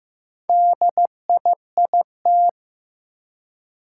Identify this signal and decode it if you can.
Morse code, heard as a keyed tone, about 15 wpm, 700 Hz. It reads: DIIT
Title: Four Challenging Letter-Number Combinations 15wpm